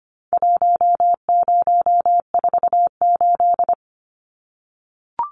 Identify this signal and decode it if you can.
Morse code, heard as a keyed tone, about 25 wpm, 700 Hz. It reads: 1048
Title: Numbers - Contest Style - 4-Digit 25wpm